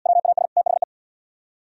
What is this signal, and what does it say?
Morse code, heard as a keyed tone, about 73 wpm, 700 Hz. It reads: WAS THE